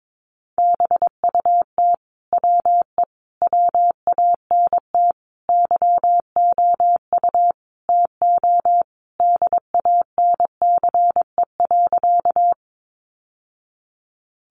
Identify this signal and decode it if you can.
Morse code, heard as a keyed tone, about 22 wpm, 700 Hz. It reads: BUT WE WANT YOU TO DANCE.
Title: Sentences from Top 700 Words 22wpm